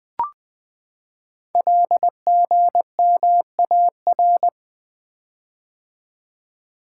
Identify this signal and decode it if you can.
Morse code, heard as a keyed tone, about 20 wpm, 700 Hz. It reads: LGMAR
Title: Code Groups - 5 Letters - 20wpm